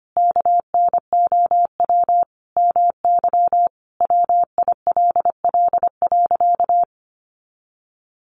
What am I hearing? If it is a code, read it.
Morse code, heard as a keyed tone, about 25 wpm, 700 Hz. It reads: KNOW MY WILL.
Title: Sentences from Top 1500 Words 25wpm